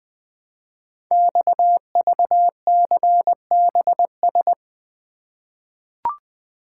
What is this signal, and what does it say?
Morse code, heard as a keyed tone, about 20 wpm, 700 Hz. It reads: XVCBS
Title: Code Groups - 5 Letters - 20wpm